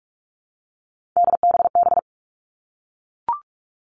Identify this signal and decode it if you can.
Morse code, heard as a keyed tone, about 44 wpm, 700 Hz. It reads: DBB
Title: All Cut Numbers - 3-Digit 45wpm